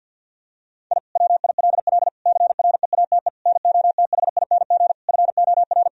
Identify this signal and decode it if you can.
Morse code, heard as a keyed tone, about 50 wpm, 700 Hz. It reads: I WILL CREATE NOTHING FOR
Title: Sentences from Top 1000 Words - Review Part 3 - 50wpm